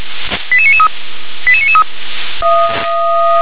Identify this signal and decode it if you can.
Five-tone selective call (ZVEI), heard as quick runs of five tones, tones of about 70 ms. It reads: 88993 88993